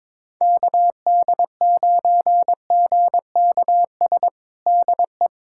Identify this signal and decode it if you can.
Morse code, heard as a keyed tone, about 23 wpm, 700 Hz. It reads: KD9GKS DE
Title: USA Callsign DE Callsign - 6 Characters - 22wpm